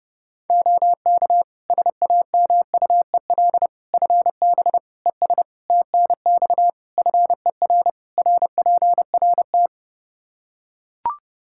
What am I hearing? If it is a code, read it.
Morse code, heard as a keyed tone, about 32 wpm, 700 Hz. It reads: OK SAMUEL FB ES TNX FER RPRT
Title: K1USN SST QSO Practice 30wpm